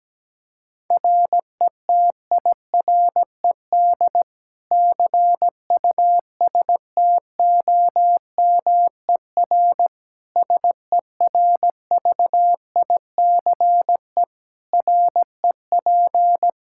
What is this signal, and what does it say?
Morse code, heard as a keyed tone, about 17 wpm, 700 Hz. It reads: RETIRED CUSTOMER SERVICE REP
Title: Practice QSO - Exchange 3 - More Info - 17wpm